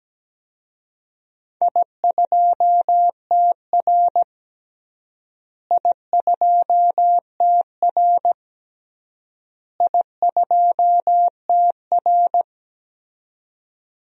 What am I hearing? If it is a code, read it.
Morse code, heard as a keyed tone, about 17 wpm, 700 Hz. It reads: I2TR I2TR I2TR
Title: DXCC-100 Call Signs - 4 Characters - Repeated 3x - 17wpm